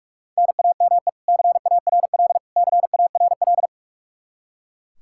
Morse code, heard as a keyed tone, about 41 wpm, 700 Hz. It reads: NAME CARL CARL
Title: K1USN SST QSO Practice 45wpm